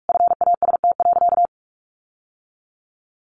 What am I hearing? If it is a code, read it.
Morse code, heard as a keyed tone, about 46 wpm, 700 Hz. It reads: FAST.
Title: Sentences from Top 1300 Words 45wpm